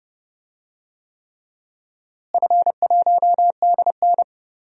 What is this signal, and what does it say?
Morse code, heard as a keyed tone, about 30 wpm, 700 Hz. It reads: F1DN
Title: DXCC-100 Call Signs - 4 Characters - Repeated 3x - 30wpm